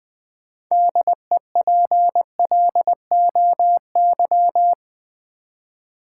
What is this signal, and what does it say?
Morse code, heard as a keyed tone, about 20 wpm, 700 Hz. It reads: DEPLOY
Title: 6 Letter Words - First 3 Letters Given - 20wpm